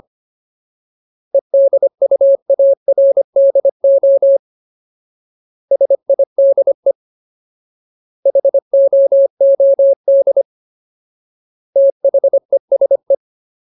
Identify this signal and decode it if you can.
Morse code, heard as a keyed tone, about 25 wpm, 550 Hz. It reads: EDUARDO SIDE HOOD THESE
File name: Ultimate Word List / 25WPM Ultimate